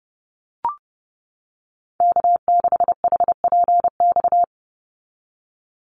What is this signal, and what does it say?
Morse code, heard as a keyed tone, about 30 wpm, 700 Hz. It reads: K6HPX
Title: NA SOTA Activator Callsigns 30wpm